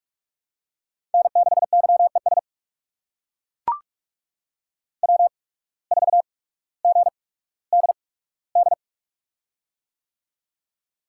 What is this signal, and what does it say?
Morse code, heard as a keyed tone, about 42 wpm, spaced as for 19 wpm, 700 Hz. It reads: N6YES W4GDD
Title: USA Call Signs - 5 Characters - 8x Character Spacing - 45wpm